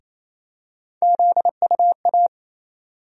Morse code, heard as a keyed tone, about 28 wpm, 700 Hz. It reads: ZUA
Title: USA Call Sign Postfixes - 3 Letters 28wpm